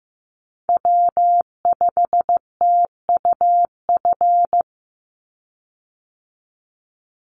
Morse code, heard as a keyed tone, about 15 wpm, 700 Hz. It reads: W5TUF
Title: Call Signs Repeated 3x - 15wpm